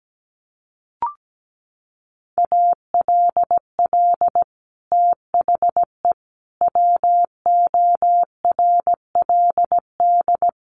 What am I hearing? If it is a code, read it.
Morse code, heard as a keyed tone, about 17 wpm, 700 Hz. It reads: ALL THE WORLD
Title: Sentences from Top 1200 Words 17wpm